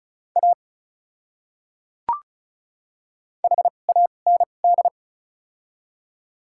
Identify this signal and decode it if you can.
Morse code, heard as a keyed tone, about 36 wpm, spaced as for 25 wpm, 700 Hz. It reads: A HAND
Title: 2x Character Spacing - Top 500 Words - 35wpm